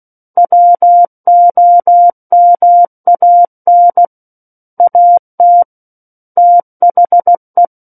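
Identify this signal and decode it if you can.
Morse code, heard as a keyed tone, about 16 wpm, 700 Hz. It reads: WOMAN AT THE
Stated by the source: Morse practice of book text at 16 wpm.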